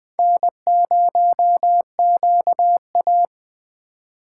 Morse code, heard as a keyed tone, about 21 wpm, 700 Hz. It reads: N0QA
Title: Speed Racing - USA 1x2 Call Signs 17wpm